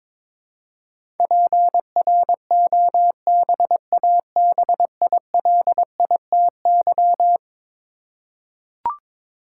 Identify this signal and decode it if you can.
Morse code, heard as a keyed tone, about 22 wpm, 700 Hz. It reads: PROBABILITY
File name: practice / 22wpm / Eleven Letter Words - 22wpm